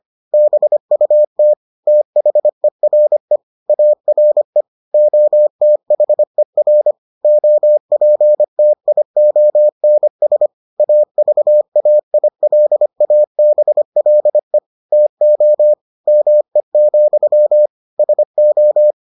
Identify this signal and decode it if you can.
Morse code, heard as a keyed tone, about 25 wpm, 600 Hz. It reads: BUT THERE ARE OTHER OPTIONS AVAILABLE TO ME, SO